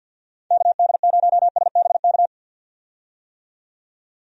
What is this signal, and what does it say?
Morse code, heard as a keyed tone, about 51 wpm, 700 Hz. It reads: KD0SBK
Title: USA Call Signs - 6 Characters - 50wpm